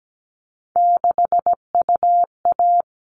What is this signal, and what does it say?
Morse code, heard as a keyed tone, about 17 wpm, 700 Hz. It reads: 6UA
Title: All Cut Numbers - 3-Digit 17wpm